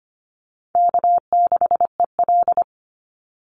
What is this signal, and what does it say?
Morse code, heard as a keyed tone, about 25 wpm, 700 Hz. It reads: K6EL
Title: NA SOTA Chaser Callsigns 25wpm